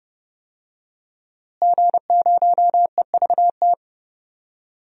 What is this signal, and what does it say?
Morse code, heard as a keyed tone, about 30 wpm, 700 Hz. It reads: G0EVT
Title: DXCC-100 Call Signs - 5 Characters - Repeated 3x - 30wpm